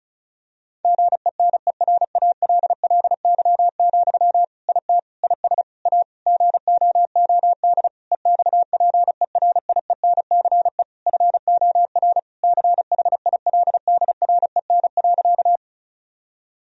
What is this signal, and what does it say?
Morse code, heard as a keyed tone, about 36 wpm, 700 Hz. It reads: GENERALLY, IT IS A GOOD EXPERIENCE FOR CHILDREN.
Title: Sentences from Top 1000 Words - Review Part 4 - 35wpm